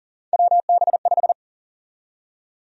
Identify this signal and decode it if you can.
Morse code, heard as a keyed tone, about 39 wpm, 700 Hz. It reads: WB5
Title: USA Call Sign Prefixes - 40wpm